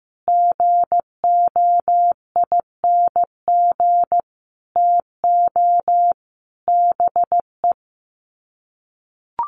Morse code, heard as a keyed tone, about 15 wpm, 700 Hz. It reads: GOING TO BE
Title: Sets of 5 Words Encore2 15wpm